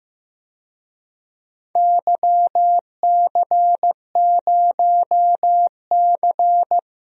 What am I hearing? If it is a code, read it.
Morse code, heard as a keyed tone, about 15 wpm, 700 Hz. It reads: YC0C
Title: DXCC-100 Call Signs - 4 Characters - 15wpm